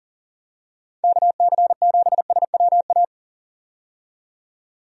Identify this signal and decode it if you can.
Morse code, heard as a keyed tone, about 37 wpm, 700 Hz. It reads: KC7SWA
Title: USA Call Signs - 6 Characters - 40wpm